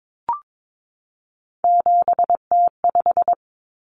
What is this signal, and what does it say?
Morse code, heard as a keyed tone, about 22 wpm, 700 Hz. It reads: 7T5
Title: Common Cut Numbers - 3-Digit 22wpm